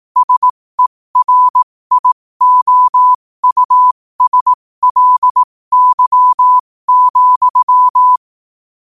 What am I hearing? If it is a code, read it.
Morse code, heard as a keyed tone, about 18 wpm, spaced as for 15 wpm, 1000 Hz. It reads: SERIOUSLY,
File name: qotdfiles / 15wpm